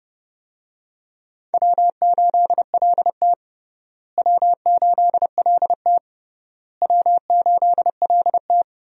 Morse code, heard as a keyed tone, about 30 wpm, 700 Hz. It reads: W8LT W8LT W8LT
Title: USA Call Signs - 4 Characters - Repeated 3x - 30wpm